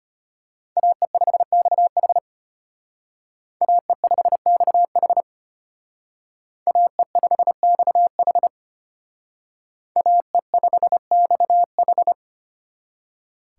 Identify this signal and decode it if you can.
Morse code, heard as a keyed tone, about 32 wpm, 700 Hz. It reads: AE5XH AE5XH AE5XH AE5XH